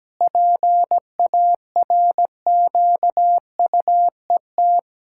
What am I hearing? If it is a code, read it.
Morse code, heard as a keyed tone, about 17 wpm, 700 Hz. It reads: PARQUET
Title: Pangram Challenge v2 - 17wpm